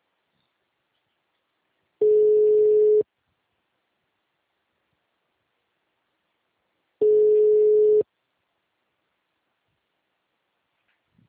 Ringback tone (Europe, 425 Hz).